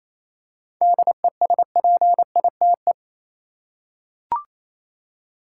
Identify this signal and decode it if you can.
Morse code, heard as a keyed tone, about 28 wpm, 700 Hz. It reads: DESPITE
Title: Seven Letter Words - 28wpm